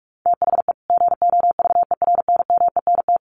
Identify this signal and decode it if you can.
Morse code, heard as a keyed tone, about 44 wpm, 700 Hz. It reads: THE GOVERNMENT
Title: Sets of 2 Words 45wpm